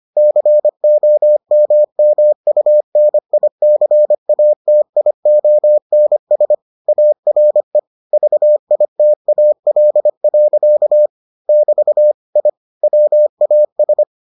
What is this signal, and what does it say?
Morse code, heard as a keyed tone, about 25 wpm, 600 Hz. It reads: COMMUNICATIONS ARE VITAL. <BT> I WAS